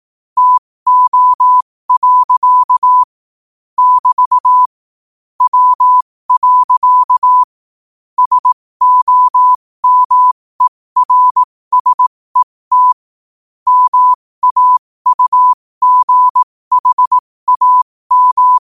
Morse code, heard as a keyed tone, about 18 wpm, spaced as for 15 wpm, 1000 Hz. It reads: TO. <BT> W. SOMERSET MAUGHAM
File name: qotdfiles / 15wpm